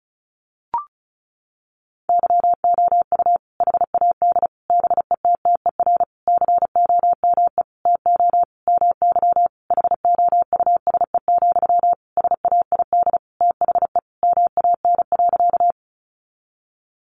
Morse code, heard as a keyed tone, about 34 wpm, 700 Hz. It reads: YOU HAD BETTER COME TO MY HOUSE, SAID THE MAN.
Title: Sentences from Top 400 Words 35wpm